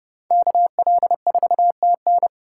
Morse code, heard as a keyed tone, about 29 wpm, 700 Hz. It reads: KL4TN